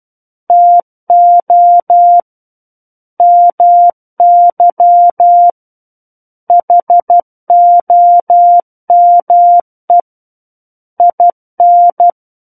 Morse code, heard as a keyed tone, about 12 wpm, 700 Hz. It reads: TO MY HOME IN
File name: Book One / 03-Chapter 3 - 12 WPM